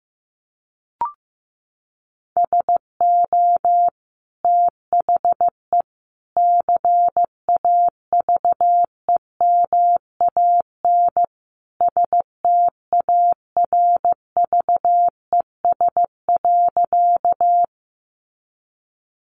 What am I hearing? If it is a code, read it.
Morse code, heard as a keyed tone, about 15 wpm, 700 Hz. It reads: SO THE CAVEMAN STARVES.
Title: A Sound Of Thunder 15wpm